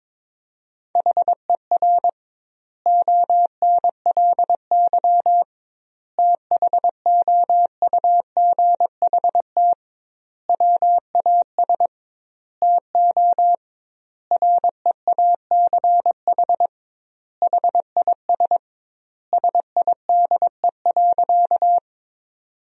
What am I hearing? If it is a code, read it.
Morse code, heard as a keyed tone, about 22 wpm, 700 Hz. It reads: HER ONLY THOUGHT WAS TO REACH HIS SIDE.
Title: 2x Word-Spacing - Sentences from Top 400 Words - 22wpm